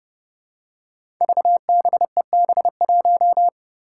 Morse code, heard as a keyed tone, about 30 wpm, 700 Hz. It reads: VBEB1
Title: Five Challenging Letter-Number Combinations 30wpm